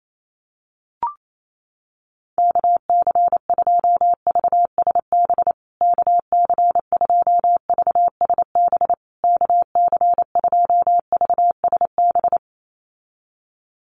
Morse code, heard as a keyed tone, about 28 wpm, 700 Hz. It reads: KC2VSB KC2VSB KC2VSB
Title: USA Call Signs Repeated 3x 28wpm